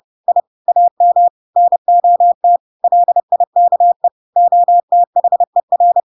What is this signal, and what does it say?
Morse code, heard as a keyed tone, about 30 wpm, 700 Hz. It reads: I AM NOT LIKE OTHER
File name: Pinocchio / Pinocchio ch8 30 30 ik1hge